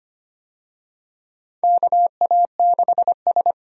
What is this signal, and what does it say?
Morse code, heard as a keyed tone, about 25 wpm, 700 Hz. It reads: KA6S
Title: USA Call Signs - 4 Characters - 25wpm